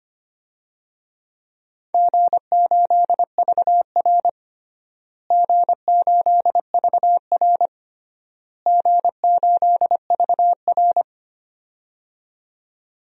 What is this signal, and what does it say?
Morse code, heard as a keyed tone, about 25 wpm, 700 Hz. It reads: G8VR G8VR G8VR
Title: DXCC-100 Call Signs - 4 Characters - Repeated 3x - 25wpm